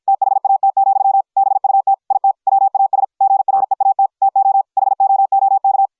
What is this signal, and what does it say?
Morse code, heard as a keyed tone, about 53 wpm, 750 Hz. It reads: THAT, BUT IT WAS GREAT TO HOOK